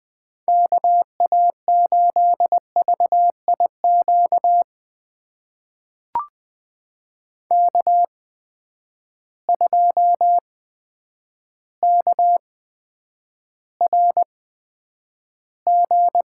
Morse code, heard as a keyed tone, about 20 wpm, spaced as for 9 wpm, 700 Hz. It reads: KA8VIQ K2KRG
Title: USA Call Signs - 8x Character Spacing - 20wpm